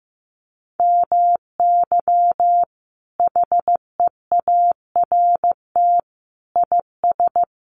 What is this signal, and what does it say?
Morse code, heard as a keyed tone, about 14 wpm, 700 Hz. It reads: MY HEART IS
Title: Sentences from Top 1200 Words 15wpm